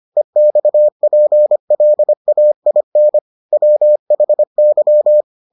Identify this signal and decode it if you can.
Morse code, heard as a keyed tone, about 25 wpm, 600 Hz. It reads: EXPLAIN WHY